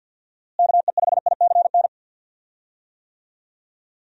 Morse code, heard as a keyed tone, about 51 wpm, 700 Hz. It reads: KE5ICN
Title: USA Call Signs - 6 Characters - Repeated 2x - 50wpm